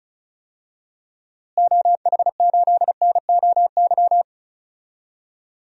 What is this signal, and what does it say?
Morse code, heard as a keyed tone, about 36 wpm, 700 Hz. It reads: OH8NOY
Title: DXCC-100 Call Signs - 6 Characters - Repeated 3x - 35wpm